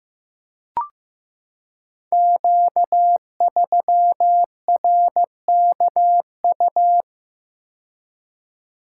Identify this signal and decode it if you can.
Morse code, heard as a keyed tone, about 15 wpm, 700 Hz. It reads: Q3RKU
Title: Code Groups - 5 Letter:Numbers - 15wpm